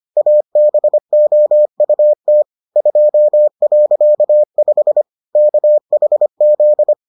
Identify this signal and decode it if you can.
Morse code, heard as a keyed tone, about 25 wpm, 600 Hz. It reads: ABOUT 2.5 KHZ